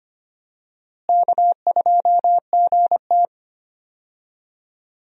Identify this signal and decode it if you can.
Morse code, heard as a keyed tone, about 25 wpm, 700 Hz. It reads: K2GT
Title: USA Call Signs - 4 Characters - Repeated 3x - 25wpm